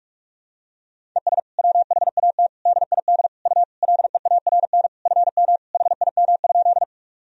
Morse code, heard as a keyed tone, about 46 wpm, 700 Hz. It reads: ES WHAT DID U LEARN FM HIM?
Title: Sentences from Top 300 Words and Abbreviations 45wpm